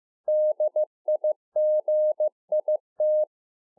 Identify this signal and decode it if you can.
Morse code, heard as a keyed tone, about 15 wpm, 610 Hz. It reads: DIGIT
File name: 2011-04-mp3 / A SMALL TRANCEIVER 2 15WPM